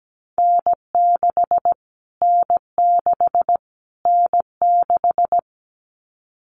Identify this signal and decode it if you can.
Morse code, heard as a keyed tone, about 17 wpm, 700 Hz. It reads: N6 N6 N6
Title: Callsign Prefixes for Top 25 DXCC - Repeated 3x - 17wpm